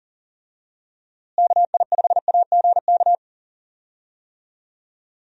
Morse code, heard as a keyed tone, about 40 wpm, 700 Hz. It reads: KI5AGK
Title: USA Call Signs - 6 Characters - Repeated 2x - 40wpm